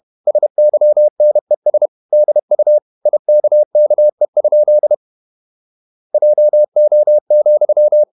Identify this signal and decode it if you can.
Morse code, heard as a keyed tone, about 31 wpm, 600 Hz. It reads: SYNES DU IKKE? JO,
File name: Ferie QRQ 31wpm